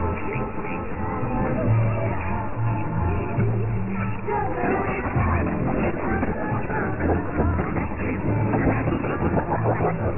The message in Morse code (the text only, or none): none